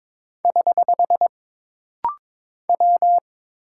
Morse code, heard as a keyed tone, about 22 wpm, 700 Hz. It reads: <HH> W
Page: This rapid-fire timing is geared to push your ICR (Instant Character Recognition) skills.